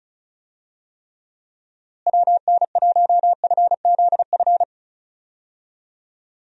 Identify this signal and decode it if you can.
Morse code, heard as a keyed tone, about 34 wpm, 700 Hz. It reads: WN1FZF
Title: USA Call Signs - 6 Characters - Repeated 2x - 35wpm